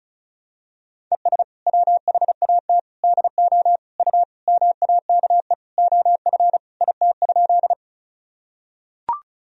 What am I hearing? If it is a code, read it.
Morse code, heard as a keyed tone, about 36 wpm, 700 Hz. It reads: ES WHAT DO U MAKE OF IT?
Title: Sentences from Top 100 Words and Abbreviations 35wpm